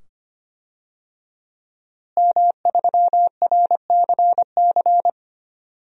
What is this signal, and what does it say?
Morse code, heard as a keyed tone, about 25 wpm, 700 Hz. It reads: M3RCC
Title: DXCC-100 Call Signs - 5 Characters - Repeated 2x - 25wpm